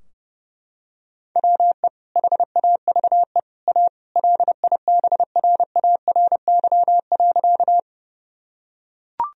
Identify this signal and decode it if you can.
Morse code, heard as a keyed tone, about 30 wpm, 700 Hz. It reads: WE HAVE A LIBRARY.
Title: Sentences from Top 1000 Words - Review Part 4 - 30wpm